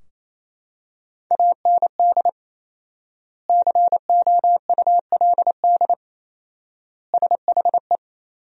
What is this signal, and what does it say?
Morse code, heard as a keyed tone, about 28 wpm, 700 Hz. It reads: AND COULD SHE
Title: 4x Word-Spacing - Sentences from Top 100 Words - 28wpm